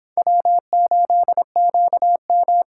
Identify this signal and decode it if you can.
Morse code, heard as a keyed tone, about 32 wpm, 700 Hz. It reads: W8QM